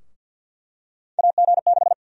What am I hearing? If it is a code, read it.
Morse code, heard as a keyed tone, about 48 wpm, 700 Hz. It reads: AG6
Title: USA Call Sign Prefixes - 50wpm